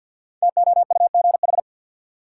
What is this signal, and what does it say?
Morse code, heard as a keyed tone, about 53 wpm, 700 Hz. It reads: TOUGH
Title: Sentences from Top 1200 Words - New Word Drill Alt - 50wpm